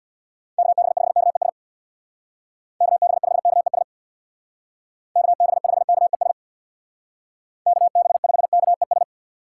Morse code, heard as a keyed tone, about 65 wpm, 700 Hz. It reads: KB5KES KB5KES KB5KES KB5KES